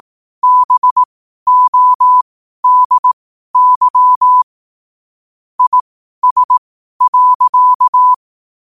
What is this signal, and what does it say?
Morse code, heard as a keyed tone, about 18 wpm, spaced as for 12 wpm, 1000 Hz. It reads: BODY IS.
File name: qotdfiles / 12wpm